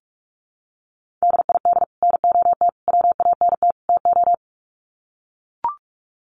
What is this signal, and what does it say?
Morse code, heard as a keyed tone, about 44 wpm, 700 Hz. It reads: DID NOT WANT TO
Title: Sets of 4 Words 45wpm